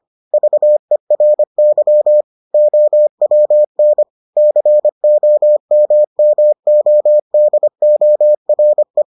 Morse code, heard as a keyed tone, about 25 wpm, 600 Hz. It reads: VERY OWN COMMODORE